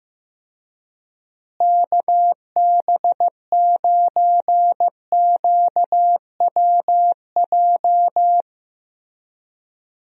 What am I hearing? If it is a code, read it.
Morse code, heard as a keyed tone, about 15 wpm, 700 Hz. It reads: KB9QWJ
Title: USA Call Signs - 6 Characters - Repeated 2x - 15wpm